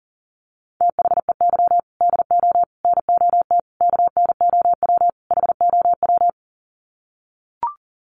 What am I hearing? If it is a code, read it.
Morse code, heard as a keyed tone, about 40 wpm, 700 Hz. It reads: THEY DO NOT KNOW HOW
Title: Sets of 5 Words Encore2 40wpm